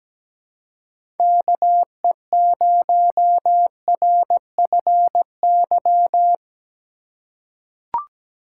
Morse code, heard as a keyed tone, about 17 wpm, 700 Hz. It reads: KE0RFY
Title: USA Call Signs - 6 Characters - 3x Character Spacing - 17wpm